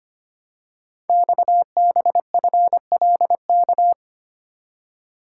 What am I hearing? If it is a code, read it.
Morse code, heard as a keyed tone, about 25 wpm, 700 Hz. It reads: XBFLK
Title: Code Groups - 5 Letters - 25wpm